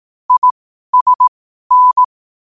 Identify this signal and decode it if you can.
Morse code, heard as a keyed tone, about 18 wpm, spaced as for 12 wpm, 1000 Hz. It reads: ISN
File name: qotdfiles / 12wpm